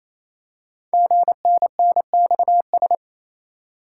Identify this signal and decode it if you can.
Morse code, heard as a keyed tone, about 27 wpm, 700 Hz. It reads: GNNXS
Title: Code Groups - 5 Letters - 28wpm